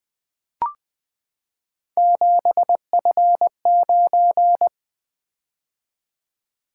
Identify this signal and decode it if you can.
Morse code, heard as a keyed tone, about 20 wpm, 700 Hz. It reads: 7F9
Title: Three Letter-Number Combinations - 20wpm